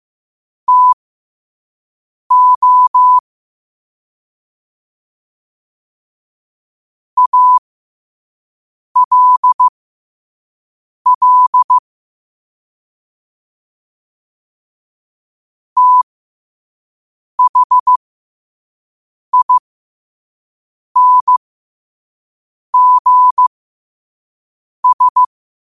Morse code, heard as a keyed tone, about 15 wpm, spaced as for 5 wpm, 1000 Hz. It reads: TO ALL THINGS
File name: qotdfiles / 5wpm